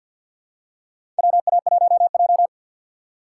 Morse code, heard as a keyed tone, about 51 wpm, 700 Hz. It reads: WA1J
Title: USA Call Signs - 4 Characters - 2x Character Spacing - 50wpm